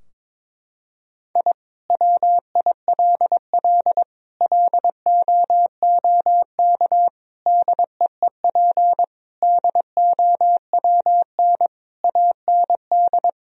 Morse code, heard as a keyed tone, about 22 wpm, 700 Hz. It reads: I WILL LOOK DEEP DOWN AND
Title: Sentences from Top 1000 Words - Review Part 3 - 22wpm